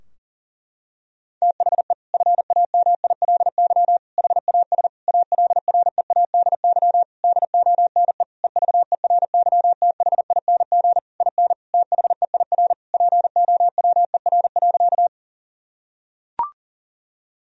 Morse code, heard as a keyed tone, about 40 wpm, 700 Hz. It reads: THE FAMILY HAS ALREADY DONE EVERYTHING IN THEIR POWER.
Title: Sentences from Top 1000 Words - Review Part 2 - 40wpm